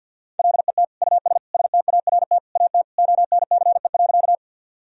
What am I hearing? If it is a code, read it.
Morse code, heard as a keyed tone, about 51 wpm, 700 Hz. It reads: LET US START AT ONCE.
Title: Sentences from Top 1000 Words - Review Part 1 - 50wpm